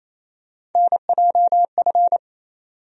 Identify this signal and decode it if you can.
Morse code, heard as a keyed tone, about 28 wpm, 700 Hz. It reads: NJF
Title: USA Call Sign Postfixes - 3 Letters 28wpm